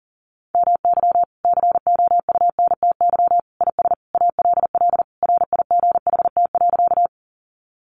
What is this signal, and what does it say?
Morse code, heard as a keyed tone, about 40 wpm, 700 Hz. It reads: MY COUNTY IS ALL RIGHT.
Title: Sentences from Top 1200 Words 40wpm